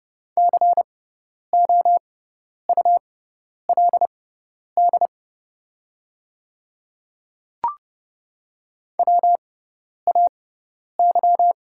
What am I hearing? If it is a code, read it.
Morse code, heard as a keyed tone, about 30 wpm, spaced as for 10 wpm, 700 Hz. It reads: COULD WAY
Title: 6x Character Spacing - Top 200 Words - 30wpm